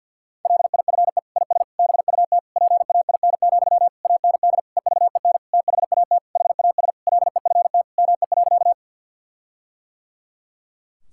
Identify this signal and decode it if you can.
Morse code, heard as a keyed tone, about 50 wpm, 700 Hz. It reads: LIFE IS BUT PAIN, AND EVEN THAT HAS LEFT ME.
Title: Sentences from Top 1600 Words 50wpm